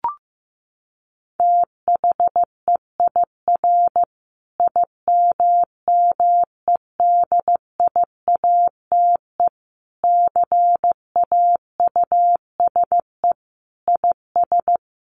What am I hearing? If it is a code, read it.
Morse code, heard as a keyed tone, about 15 wpm, 700 Hz. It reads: THEIR IMMEDIATE CAUSE IS
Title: Sentences from Top 1200 Words 15wpm